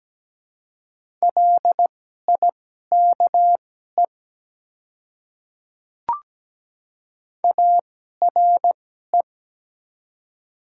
Morse code, heard as a keyed tone, about 17 wpm, spaced as for 12 wpm, 700 Hz. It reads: LIKE ARE
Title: 2x Character Spacing - Top 200 Words - 17wpm